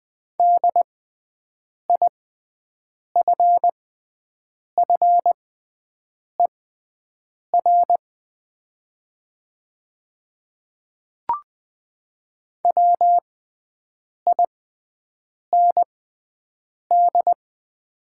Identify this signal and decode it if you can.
Morse code, heard as a keyed tone, about 20 wpm, spaced as for 7 wpm, 700 Hz. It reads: DIFFER WIND
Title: 6x Character Spacing - Top 500 Words - 20wpm